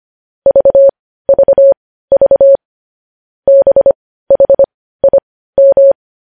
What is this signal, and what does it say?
Morse code, heard as a keyed tone, about 25 wpm, spaced as for 15 wpm, 550 Hz. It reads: VVV BHIM
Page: News Headlines in Morse Code at 15 WPM